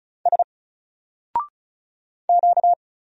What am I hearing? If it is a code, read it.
Morse code, heard as a keyed tone, about 35 wpm, 700 Hz. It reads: S Q
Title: Single Character (Letter:Number:Punctuation:Prosign) 35wpm